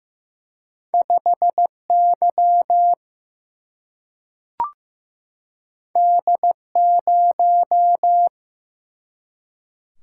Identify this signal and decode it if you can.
Morse code, heard as a keyed tone, about 15 wpm, 700 Hz. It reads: 5 Y D 0
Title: Single-Letter Single-Number 15wpm